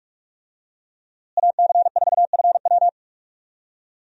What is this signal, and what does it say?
Morse code, heard as a keyed tone, about 43 wpm, 700 Hz. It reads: AC4FW
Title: USA Call Signs - 5 Characters - Repeated 3x - 45wpm